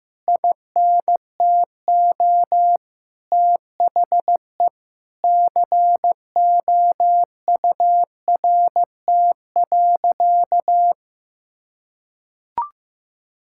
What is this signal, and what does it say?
Morse code, heard as a keyed tone, about 15 wpm, 700 Hz. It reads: INTO THE COURT.
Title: Sentences from Top 1600 Words 15wpm